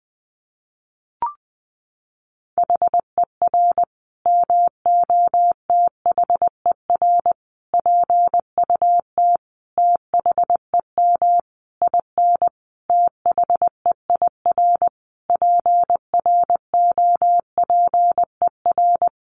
Morse code, heard as a keyed tone, about 20 wpm, 700 Hz. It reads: HER MOTHER PUT THEM IN THEIR PROPER
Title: Sentences from Top 800 Words 20wpm